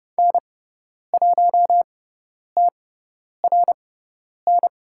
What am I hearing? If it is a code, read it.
Morse code, heard as a keyed tone, about 30 wpm, spaced as for 10 wpm, 700 Hz. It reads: N1TRN
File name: Lesson 16g - Letter c - Call Signs 30wpm